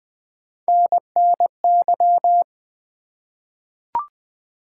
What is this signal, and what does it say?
Morse code, heard as a keyed tone, about 20 wpm, 700 Hz. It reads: NNY
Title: ARRL Sections v2 - 20wpm